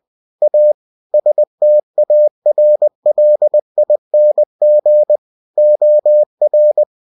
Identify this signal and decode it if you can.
Morse code, heard as a keyed tone, about 20 wpm, 600 Hz. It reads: A STARLING OR
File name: cw-books / winnie-the-pooh / winnie-20wpm-page-0069